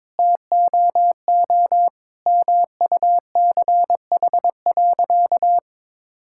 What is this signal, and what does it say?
Morse code, heard as a keyed tone, about 21 wpm, 700 Hz. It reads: TOO MUCH.
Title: Sentences from Top 1000 Words - Review Part 5 - 22wpm